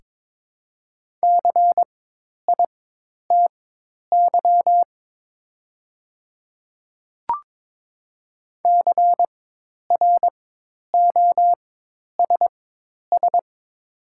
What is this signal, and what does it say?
Morse code, heard as a keyed tone, about 21 wpm, spaced as for 10 wpm, 690 Hz. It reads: CITY CROSS
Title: 4x Character Spacing - Top 300 Words - 22wpm